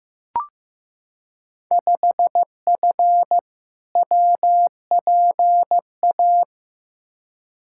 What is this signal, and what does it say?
Morse code, heard as a keyed tone, about 15 wpm, 700 Hz. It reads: 5F WPA
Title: Field Day Exchanges Encore - 15wpm